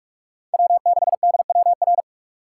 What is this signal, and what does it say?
Morse code, heard as a keyed tone, about 46 wpm, 700 Hz. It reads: W6DWR
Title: USA Call Signs - 5 Characters - 2x Character Spacing - 45wpm